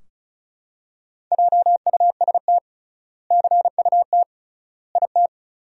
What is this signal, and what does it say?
Morse code, heard as a keyed tone, about 36 wpm, 700 Hz. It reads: JUST CUT IT
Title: 3x Word-Spacing - Sentences from Top 300 Words - 35wpm